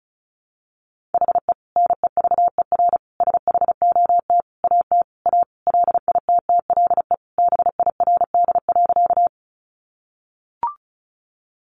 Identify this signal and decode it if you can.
Morse code, heard as a keyed tone, about 34 wpm, 700 Hz. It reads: HE NEVER SHOT AT A LITTLE BIRD.
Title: Sentences from Top 1100 Words 35wpm